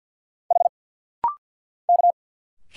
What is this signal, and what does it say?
Morse code, heard as a keyed tone, about 50 wpm, 690 Hz. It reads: H K
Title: Single Letters - Rapid-Fire 50wpm